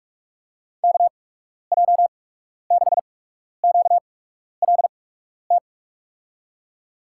Morse code, heard as a keyed tone, about 44 wpm, spaced as for 12 wpm, 700 Hz. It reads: KJ6QLT
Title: USA Call Signs - 6 Characters - 8x Character Spacing - 45wpm